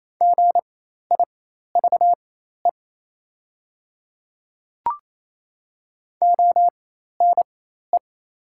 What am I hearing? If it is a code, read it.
Morse code, heard as a keyed tone, about 28 wpm, spaced as for 13 wpm, 700 Hz. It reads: GIVE ONE
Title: 4x Character Spacing - Top 500 Words - 28wpm